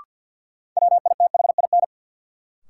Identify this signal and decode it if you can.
Morse code, heard as a keyed tone, about 52 wpm, 700 Hz. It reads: WITHIN
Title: Sentences from Top 700 Words - New Word Drill - 50wpm